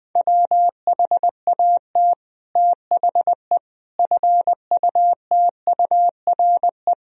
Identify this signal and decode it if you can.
Morse code, heard as a keyed tone, about 19 wpm, 700 Hz. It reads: WHAT THE FUTURE
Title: Sentences from Top 1000 Words - Review Part 2 - 20wpm